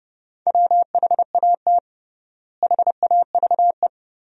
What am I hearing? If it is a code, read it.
Morse code, heard as a keyed tone, about 30 wpm, 700 Hz. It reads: WHAT HAVE
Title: 3x Word-Spacing - Sentences from Top 500 Words - 30wpm